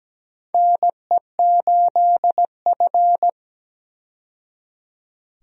Morse code, heard as a keyed tone, about 17 wpm, 700 Hz. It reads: NE8F
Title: USA Call Signs - 4 Characters - 17wpm